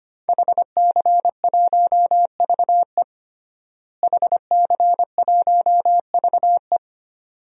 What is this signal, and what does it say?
Morse code, heard as a keyed tone, about 25 wpm, 700 Hz. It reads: HC1VE HC1VE
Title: DXCC-100 Call Signs - 5 Characters - Repeated 2x - 25wpm